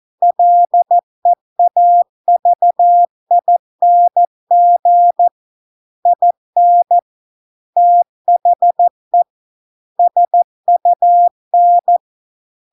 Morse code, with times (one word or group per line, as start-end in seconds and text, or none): 0.2-5.3 LEAVING
6.1-7.0 IN
7.8-9.2 THE
10.0-12.0 SUN